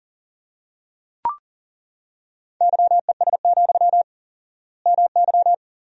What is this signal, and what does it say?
Morse code, heard as a keyed tone, about 40 wpm, 700 Hz. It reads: YES, MY
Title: 4x Word-Spacing - Sentences from Top 500 Words - 40wpm